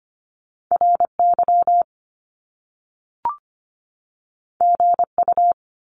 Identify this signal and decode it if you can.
Morse code, heard as a keyed tone, about 25 wpm, 700 Hz. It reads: RY GU
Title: Top Two Letter Combinations in Words 25wpm